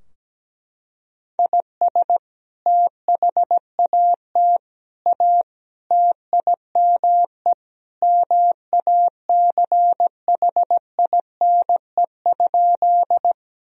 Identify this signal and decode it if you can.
Morse code, heard as a keyed tone, about 17 wpm, 700 Hz. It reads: IS THAT A TIME MACHINE?
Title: Sentences from Top 1000 Words - Review Part 2 - 17wpm